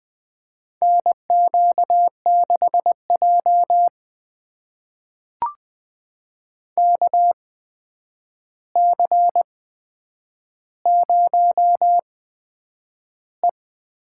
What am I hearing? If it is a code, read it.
Morse code, heard as a keyed tone, about 20 wpm, spaced as for 9 wpm, 700 Hz. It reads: NQ6J KC0E
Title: USA Call Signs - 8x Character Spacing - 20wpm